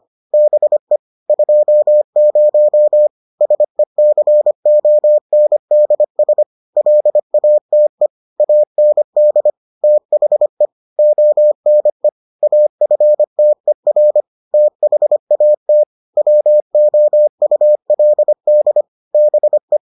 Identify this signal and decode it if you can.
Morse code, heard as a keyed tone, about 25 wpm, 600 Hz. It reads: BE 20 SECONDS LATE AND THE ONE AFTER THAT WOULD BE